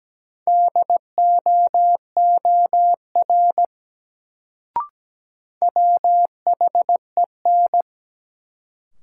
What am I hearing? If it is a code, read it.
Morse code, heard as a keyed tone, about 17 wpm, 700 Hz. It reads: DOOR WHEN
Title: Sets of 2 Words - New Word Drill - 17wpm